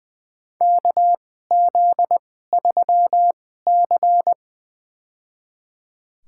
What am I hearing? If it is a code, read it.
Morse code, heard as a keyed tone, about 20 wpm, spaced as for 15 wpm, 700 Hz. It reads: KZ3C
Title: USA Call Signs - 2x Character Spacing - 20wpm